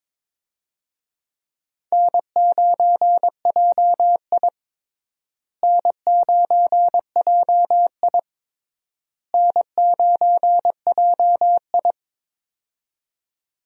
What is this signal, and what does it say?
Morse code, heard as a keyed tone, about 22 wpm, 700 Hz. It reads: N9JI N9JI N9JI
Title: USA Call Signs - 4 Characters - Repeated 3x - 22wpm